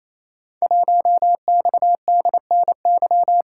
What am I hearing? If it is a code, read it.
Morse code, heard as a keyed tone, about 27 wpm, 700 Hz. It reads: 1XDNY
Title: Code Groups - 5 Letter:Numbers - 28wpm